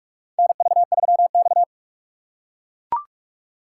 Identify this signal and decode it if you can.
Morse code, heard as a keyed tone, about 44 wpm, 690 Hz. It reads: NV3X
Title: USA Call Signs - 4 Characters - 2x Character Spacing - 45wpm